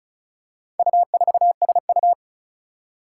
Morse code, heard as a keyed tone, about 36 wpm, 700 Hz. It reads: U4SU
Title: Four Challenging Letter-Number Combinations 35wpm